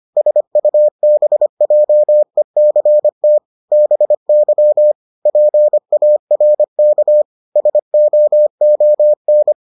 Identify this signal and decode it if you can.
Morse code, heard as a keyed tone, about 25 wpm, 600 Hz. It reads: SUBJECT BY PARK SOON